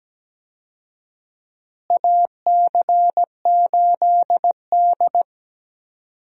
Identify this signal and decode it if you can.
Morse code, heard as a keyed tone, about 17 wpm, 700 Hz. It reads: AC8D
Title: USA Call Signs - 4 Characters - Repeated 3x - 17wpm